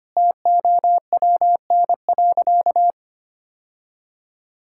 Morse code, heard as a keyed tone, about 25 wpm, 700 Hz. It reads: TOWN.
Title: Sentences from Top 1000 Words - Review Part 6 - 25wpm